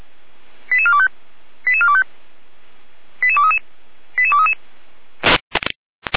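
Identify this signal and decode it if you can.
Five-tone selective call (ZVEI), heard as quick runs of five tones, tones of about 70 ms. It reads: 88425 88425 88233 88233
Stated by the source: ZVEI five-tone selective call received on SDR.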